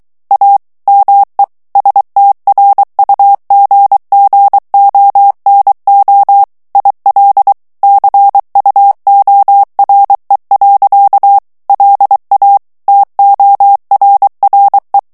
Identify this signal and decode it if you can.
Morse code, heard as a keyed tone, about 23 wpm, 800 Hz. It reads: A ME STRUGGONO IL CUORE. LA TORRE
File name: morse mp3 / ITALIEN texte clair 1 vitesse 22 mots minute